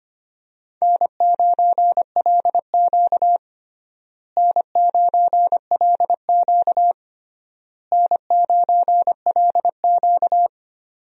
Morse code, heard as a keyed tone, about 25 wpm, 700 Hz. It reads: N9LQ N9LQ N9LQ
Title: USA Call Signs - 4 Characters - Repeated 3x - 25wpm